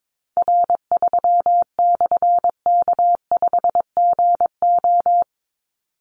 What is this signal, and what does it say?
Morse code, heard as a keyed tone, about 22 wpm, 700 Hz. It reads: R3/K5GO
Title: Call Signs - 22wpm